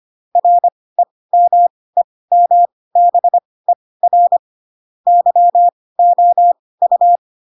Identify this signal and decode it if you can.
Morse code, heard as a keyed tone, about 25 wpm, spaced as for 18 wpm, 700 Hz. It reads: REMEMBER YOU
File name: Pinocchio / Pinocchio ch2 12 25 ik1hge